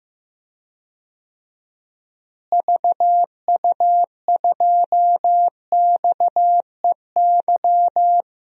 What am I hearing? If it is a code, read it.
Morse code, heard as a keyed tone, about 16 wpm, 700 Hz. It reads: VU2XEY
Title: DXCC-100 Call Signs - 6 Characters - Repeated 2x - 15wpm